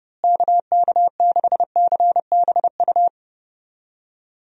Morse code, heard as a keyed tone, about 30 wpm, 700 Hz. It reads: KK6CBU
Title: USA Call Signs - 6 Characters - Repeated 3x - 30wpm